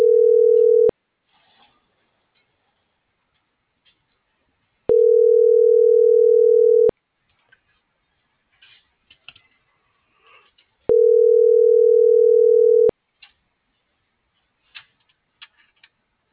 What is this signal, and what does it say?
Ringback tone (North America), 440 + 480 Hz.